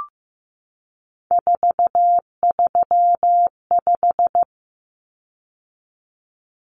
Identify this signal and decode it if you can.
Morse code, heard as a keyed tone, about 15 wpm, 700 Hz. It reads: 435
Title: Common Cut Numbers - 3-Digit 15wpm